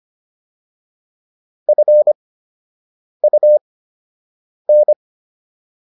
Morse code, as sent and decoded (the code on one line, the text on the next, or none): ..-. ..- -.
FUN